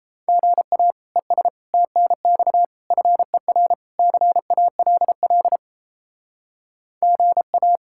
Morse code, heard as a keyed tone, about 34 wpm, 700 Hz. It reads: GA ES TNX FER CALL GA
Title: Speed-Racing Common QSO Phrases - 28wpm